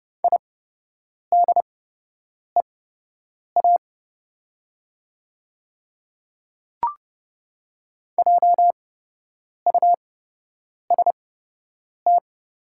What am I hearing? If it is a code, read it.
Morse code, heard as a keyed tone, about 29 wpm, spaced as for 8 wpm, 700 Hz. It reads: IDEA JUST
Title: 8x Character Spacing - Top 500 Words - 30wpm